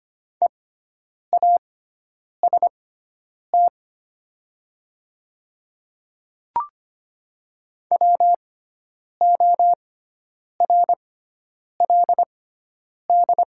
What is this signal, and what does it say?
Morse code, heard as a keyed tone, about 25 wpm, spaced as for 9 wpm, 700 Hz. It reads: EAST WORLD
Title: 6x Character Spacing - Top 500 Words - 25wpm